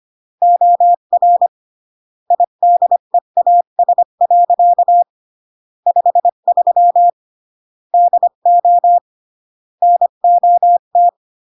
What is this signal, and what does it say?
Morse code, heard as a keyed tone, about 25 wpm, 700 Hz. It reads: OR IDEAS. 53 DO NOT
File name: CW / jodorowsky / 82 commandments 25-20